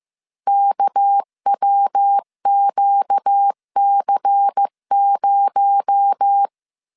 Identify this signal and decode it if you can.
Morse code, heard as a keyed tone, about 15 wpm, 800 Hz. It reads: KWQC0
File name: CW / DET / 1 9 1500C S C 14wpm 3245